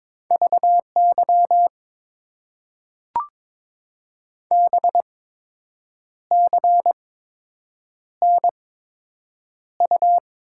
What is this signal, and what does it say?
Morse code, heard as a keyed tone, about 22 wpm, spaced as for 7 wpm, 700 Hz. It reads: VY BCNU
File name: Top 90 QSO Elements - 8x Character Spacing - 22wpm